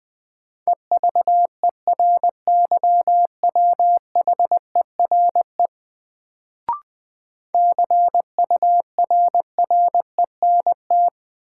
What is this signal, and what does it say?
Morse code, heard as a keyed tone, about 20 wpm, 700 Hz. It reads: EVERYWHERE CURRENT
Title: Sentences from Top 1600 Words - New Word Drill Alt - 20wpm